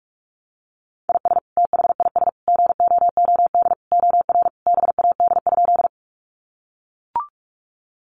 Morse code, heard as a keyed tone, about 44 wpm, 700 Hz. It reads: IS THIS GOOD OR BAD?
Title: Sentences from Top 700 Words 45wpm